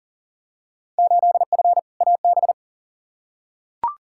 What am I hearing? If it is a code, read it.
Morse code, heard as a keyed tone, about 40 wpm, 700 Hz. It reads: 8F AB
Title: Field Day Exchanges Encore - 40wpm